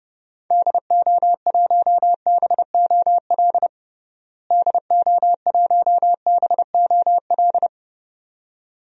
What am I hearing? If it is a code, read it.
Morse code, heard as a keyed tone, about 31 wpm, 700 Hz. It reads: DO1BOL DO1BOL
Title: DXCC-100 Call Signs - 6 Characters - Repeated 2x - 30wpm